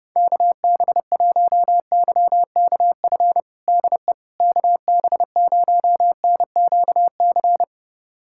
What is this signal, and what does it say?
Morse code, heard as a keyed tone, about 32 wpm, 700 Hz. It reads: KB1YKF DE KB0NQC
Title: USA Callsign DE Callsign - 6 Characters - 30wpm